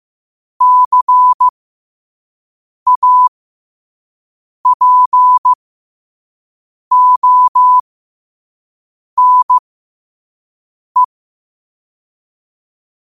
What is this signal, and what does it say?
Morse code, heard as a keyed tone, about 15 wpm, spaced as for 5 wpm, 1000 Hz. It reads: CAPONE
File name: qotdfiles / 5wpm